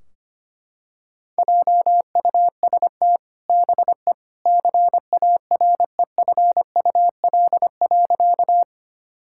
Morse code, heard as a keyed tone, about 25 wpm, 700 Hz. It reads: JUST BE CAREFUL.
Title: Sentences from Top 1600 Words 25wpm